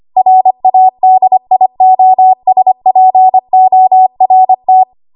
Morse code, heard as a keyed tone, about 25 wpm, 750 Hz. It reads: RADIOSPORT
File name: Archive / 25 WPM / 25WPM